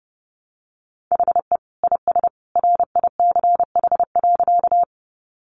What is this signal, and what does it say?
Morse code, heard as a keyed tone, about 30 wpm, 700 Hz. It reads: HE IS RICH.
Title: Sentences from Top 1100 Words 30wpm